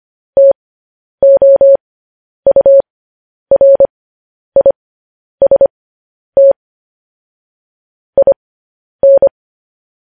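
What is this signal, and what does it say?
Morse code, heard as a keyed tone, about 25 wpm, spaced as for 10 wpm, 550 Hz. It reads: TOURIST IN
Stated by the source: Morse code news headlines broadcast.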